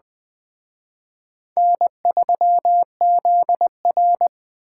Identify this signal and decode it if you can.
Morse code, heard as a keyed tone, about 20 wpm, 700 Hz. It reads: N3ZR